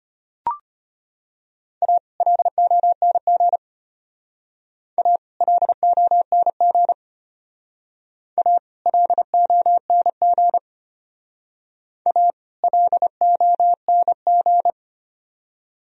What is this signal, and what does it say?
Morse code, heard as a keyed tone, about 33 wpm, 700 Hz. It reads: A LONG A LONG A LONG A LONG